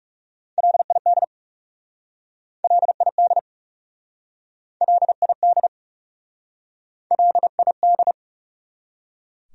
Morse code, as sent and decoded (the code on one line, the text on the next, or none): .-.. .. -.. / .-.. .. -.. / .-.. .. -.. / .-.. .. -..
LID LID LID LID